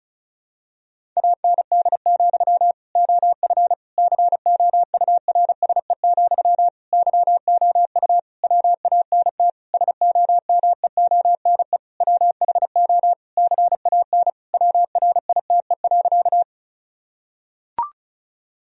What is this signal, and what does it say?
Morse code, heard as a keyed tone, about 36 wpm, 700 Hz. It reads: AND, OF COURSE, YOU WANT SOMEONE WHO CAN WRITE.
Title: Sentences from Top 1000 Words - Review Part 1 - 35wpm